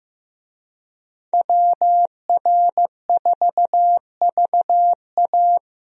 Morse code, heard as a keyed tone, about 15 wpm, 700 Hz. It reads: WR4VA
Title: USA Call Signs - 5 Characters - Repeated 2x - 15wpm